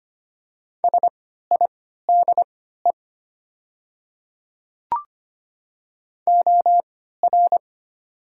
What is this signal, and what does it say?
Morse code, heard as a keyed tone, about 25 wpm, spaced as for 14 wpm, 700 Hz. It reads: SIDE OR
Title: 3x Character Spacing - Top 100 Words - 25wpm